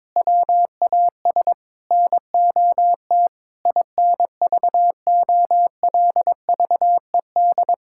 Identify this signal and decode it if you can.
Morse code, heard as a keyed tone, about 22 wpm, 700 Hz. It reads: WAS NOT INVOLVED
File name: practice / 22wpm / Sentences with 8-Word Phrases - 22wpm